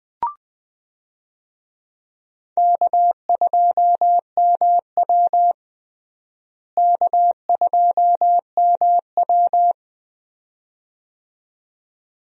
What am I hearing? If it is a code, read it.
Morse code, heard as a keyed tone, about 21 wpm, 700 Hz. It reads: K2MW K2MW
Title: USA Call Signs - 4 Characters - Repeated 2x - 20wpm